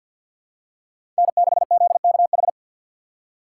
Morse code, heard as a keyed tone, about 51 wpm, 700 Hz. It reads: N6ZKH
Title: USA Call Signs - 5 Characters - 2x Character Spacing - 50wpm